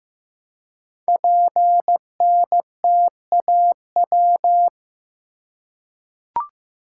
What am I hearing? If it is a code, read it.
Morse code, heard as a keyed tone, about 15 wpm, 700 Hz. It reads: PNTAW
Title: Code Groups - 5 Letters - 15wpm